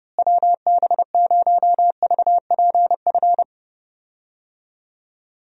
Morse code, heard as a keyed tone, about 30 wpm, 700 Hz. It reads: WB0VPF
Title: USA Call Signs - 6 Characters - Repeated 3x - 30wpm